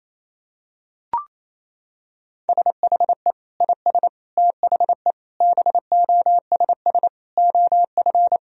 Morse code, heard as a keyed tone, about 28 wpm, 700 Hz. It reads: SHE IS THE BOSS OF
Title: Sentences from Top 1000 Words - Review Part 4 - 28wpm